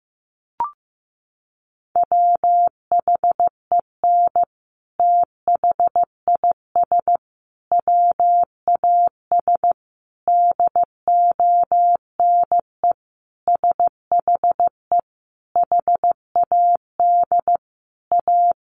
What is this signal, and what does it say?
Morse code, heard as a keyed tone, about 15 wpm, 700 Hz. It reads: WHEN THIS WAS DONE SHE HAD A
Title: Sentences from Top 900 Words 15wpm